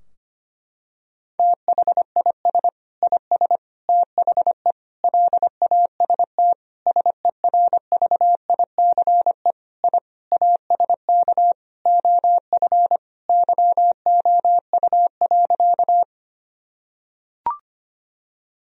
Morse code, heard as a keyed tone, about 25 wpm, 700 Hz. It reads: THIS IS THE LAST SERVICE I ASK OF YOU.
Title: Sentences from Top 1000 Words - Review Part 4 - 25wpm